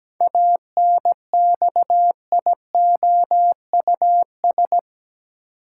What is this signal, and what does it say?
Morse code, heard as a keyed tone, about 17 wpm, 700 Hz. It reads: ANXIOUS
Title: Seven Letter Words - 17wpm